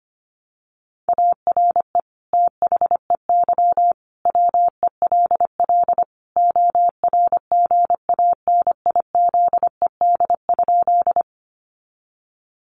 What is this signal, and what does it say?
Morse code, heard as a keyed tone, about 25 wpm, 700 Hz. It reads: ARE THEY WELL ORGANIZED?
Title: Sentences from Top 1300 Words 25wpm